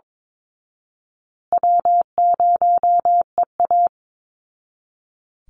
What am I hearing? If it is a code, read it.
Morse code, heard as a keyed tone, about 22 wpm, 690 Hz. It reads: W0EA
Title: Easy Call Signs Repeated 2x - 22wpm